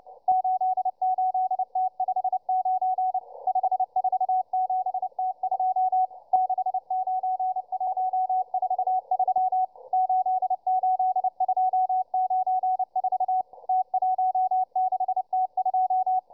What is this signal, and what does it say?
Morse code, heard as a keyed tone, about 29 wpm, 750 Hz, with radio noise. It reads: 88T59 547T2 69143 88294 T16T2